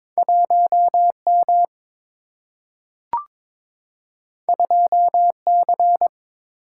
Morse code, heard as a keyed tone, about 22 wpm, 700 Hz. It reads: 1M 2C
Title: Single-Letter Single-Number 22wpm